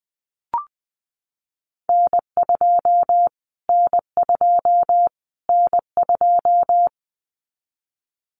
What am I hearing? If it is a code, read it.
Morse code, heard as a keyed tone, about 20 wpm, 700 Hz. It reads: N2 N2 N2
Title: Callsign Prefixes for Top 25 DXCC - Repeated 3x - 20wpm